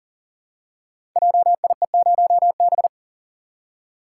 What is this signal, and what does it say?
Morse code, heard as a keyed tone, about 40 wpm, 690 Hz. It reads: JIE0B
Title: Code Groups - 5 Letter:Numbers - 40wpm